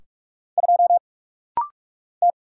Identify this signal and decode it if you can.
Morse code, heard as a keyed tone, about 44 wpm, 700 Hz. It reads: 2 T
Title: Single Letter-Number Alt - Rapid-Fire 45wpm